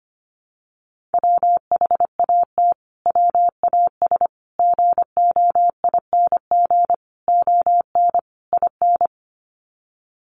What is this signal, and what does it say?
Morse code, heard as a keyed tone, about 25 wpm, 700 Hz. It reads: WHAT WAS GOING ON IN
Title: Sets of 5 Words Encore 25wpm